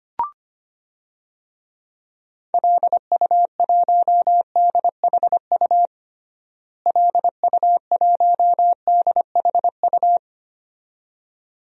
Morse code, heard as a keyed tone, about 25 wpm, 700 Hz. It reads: LU1DHU LU1DHU
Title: DXCC-100 Call Signs - 6 Characters - Repeated 2x - 25wpm